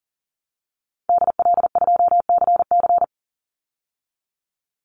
Morse code, heard as a keyed tone, about 40 wpm, 700 Hz. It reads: DL2CC
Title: Call Signs Repeated 2x - 40wpm